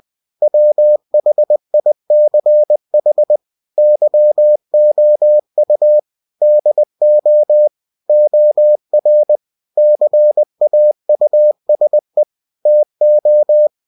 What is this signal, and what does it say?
Morse code, heard as a keyed tone, about 20 wpm, 600 Hz. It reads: WHICH YOU DO OR CAUSE TO
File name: cw-books / winnie-the-pooh / winnie-20wpm-page-0120